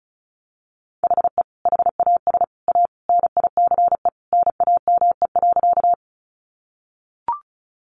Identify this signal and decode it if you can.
Morse code, heard as a keyed tone, about 34 wpm, 700 Hz. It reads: HE HAS A NICE NAME.
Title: Sentences from Top 1100 Words 35wpm